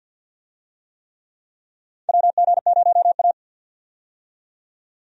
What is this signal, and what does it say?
Morse code, heard as a keyed tone, about 51 wpm, 700 Hz. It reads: WG0A
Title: USA Call Signs - 4 Characters - 50wpm